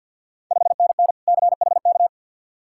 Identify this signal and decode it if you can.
Morse code, heard as a keyed tone, about 52 wpm, 700 Hz. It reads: 5NN CHK
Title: Ontario QSO Party Exchange 50wpm